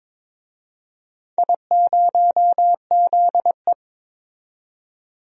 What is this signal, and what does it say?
Morse code, heard as a keyed tone, about 22 wpm, 700 Hz. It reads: I0ZE
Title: DXCC-100 Call Signs - 4 Characters - 22wpm